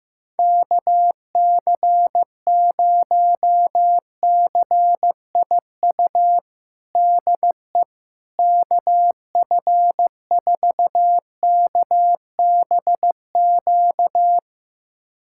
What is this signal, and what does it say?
Morse code, heard as a keyed tone, about 16 wpm, 700 Hz. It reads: KC0CIU DE KF4KBQ
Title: USA Callsign DE Callsign - 6 Characters - 15wpm